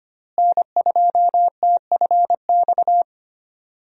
Morse code, heard as a keyed tone, about 25 wpm, 700 Hz. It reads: N2TFX
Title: USA Call Signs - 5 Characters - Repeated 2x - 25wpm